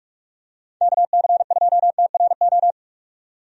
Morse code, heard as a keyed tone, about 46 wpm, 700 Hz. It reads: KC2TRO
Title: USA Call Signs - 6 Characters - 45wpm